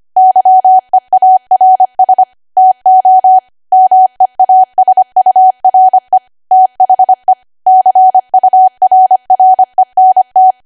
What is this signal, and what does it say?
Morse code, heard as a keyed tone, about 25 wpm, 750 Hz. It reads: YEARS TO MEASURE THE CURRENT
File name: Archive / 25 WPM / 25WPM